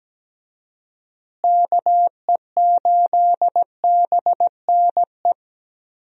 Morse code, heard as a keyed tone, about 17 wpm, 700 Hz. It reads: KE8BNE
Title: USA Call Signs - 6 Characters - Repeated 2x - 17wpm